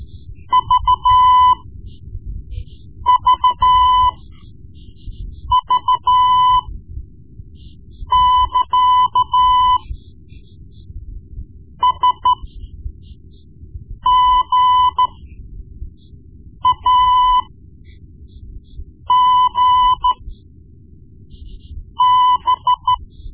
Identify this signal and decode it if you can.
Morse code, heard as a keyed tone, about 13 wpm, spaced as for 5 wpm, 1000 Hz. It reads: VVV <KA> SGAGB
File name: Morse Lessons / Morse Lessons / Tape-1B